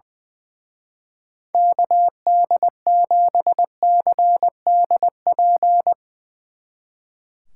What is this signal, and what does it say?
Morse code, heard as a keyed tone, about 21 wpm, 700 Hz. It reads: KD7CDP
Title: Speed Racing - USA 2x3 Call Signs 20wpm